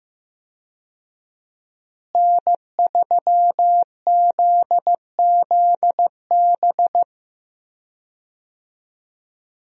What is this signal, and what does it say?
Morse code, heard as a keyed tone, about 15 wpm, 700 Hz. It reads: N3ZZB
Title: USA Call Signs - 5 Characters - Repeated 3x - 15wpm